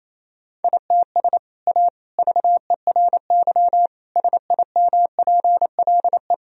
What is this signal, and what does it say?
Morse code, heard as a keyed tone, about 28 wpm, 700 Hz. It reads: ITS A VERY SIMPLE
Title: Sentences from Top 1000 Words - Review Part 3 - 28wpm